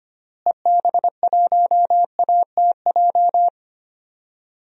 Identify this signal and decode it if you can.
Morse code, heard as a keyed tone, about 25 wpm, 700 Hz. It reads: EB1ATJ
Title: DXCC-100 Call Signs - 6 Characters - Repeated 2x - 25wpm